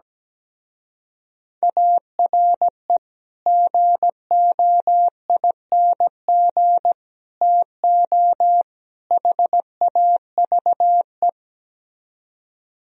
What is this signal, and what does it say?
Morse code, heard as a keyed tone, about 17 wpm, 700 Hz. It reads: ARE GOING TO HAVE
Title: Speed-Racing - Sets of 4 Words 17wpm